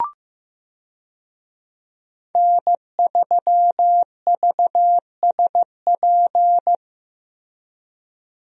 Morse code, heard as a keyed tone, about 16 wpm, 700 Hz. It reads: N3VSP
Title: USA Call Signs - 5 Characters - Repeated 2x - 15wpm